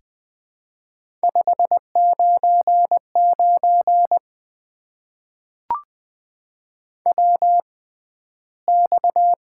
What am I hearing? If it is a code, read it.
Morse code, heard as a keyed tone, about 20 wpm, spaced as for 12 wpm, 700 Hz. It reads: 599 WX
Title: Top 90 QSO Elements - 6x Character Spacing - 20wpm